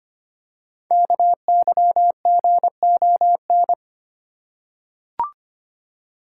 Morse code, heard as a keyed tone, about 25 wpm, 700 Hz. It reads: KYGON
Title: Code Groups - 5 Letters - 25wpm